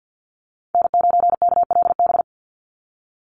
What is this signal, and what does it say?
Morse code, heard as a keyed tone, about 50 wpm, 700 Hz. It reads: N9KLB
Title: USA Call Signs 50wpm